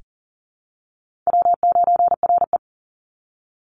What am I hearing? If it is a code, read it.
Morse code, heard as a keyed tone, about 40 wpm, 700 Hz. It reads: W9RE
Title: Easy Call Signs Repeated 2x - 40wpm